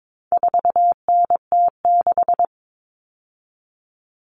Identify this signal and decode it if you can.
Morse code, heard as a keyed tone, about 22 wpm, 700 Hz. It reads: 4NT6
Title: All Cut Numbers - 4-Digit 22wpm